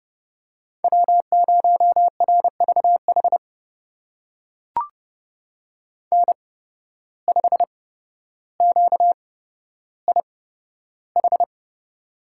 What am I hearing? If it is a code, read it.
Morse code, heard as a keyed tone, about 31 wpm, spaced as for 13 wpm, 700 Hz. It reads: W0RVH N5QIH
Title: USA Call Signs - 8x Character Spacing - 30wpm